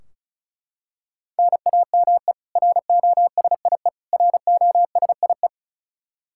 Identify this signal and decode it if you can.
Morse code, heard as a keyed tone, about 34 wpm, 700 Hz. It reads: NAME ROSIE ROSIE
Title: K1USN SST QSO Practice 35wpm